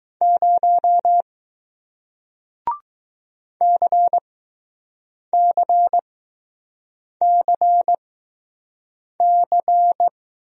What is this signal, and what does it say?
Morse code, heard as a keyed tone, about 20 wpm, spaced as for 6 wpm, 700 Hz. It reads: 0 CCCC